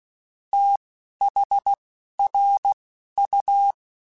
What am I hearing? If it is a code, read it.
Morse code, heard as a keyed tone, about 16 wpm, spaced as for 11 wpm, 800 Hz. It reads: THRU
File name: QSOs / QSOs-mp3s / WD0FBT 1@10